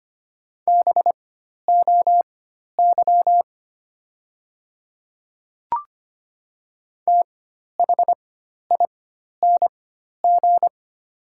Morse code, heard as a keyed tone, about 25 wpm, spaced as for 12 wpm, 700 Hz. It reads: BOY THING
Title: 4x Character Spacing - Top 200 Words - 25wpm